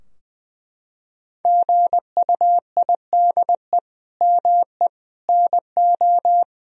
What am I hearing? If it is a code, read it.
Morse code, heard as a keyed tone, about 20 wpm, 700 Hz. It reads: GUIDE ME NO
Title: Sentences from Top 1000 Words - Review Part 3 - 20wpm